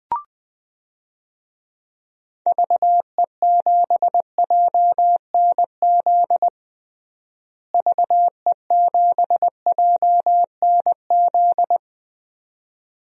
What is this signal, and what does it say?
Morse code, heard as a keyed tone, about 20 wpm, 700 Hz. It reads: VE7JNZ VE7JNZ
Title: DXCC-100 Call Signs - 6 Characters - Repeated 2x - 20wpm